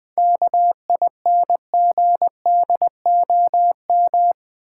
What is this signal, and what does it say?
Morse code, heard as a keyed tone, about 20 wpm, 700 Hz. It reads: KINGDOM
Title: 7 Letter Words - First 3 Letters Given - 20wpm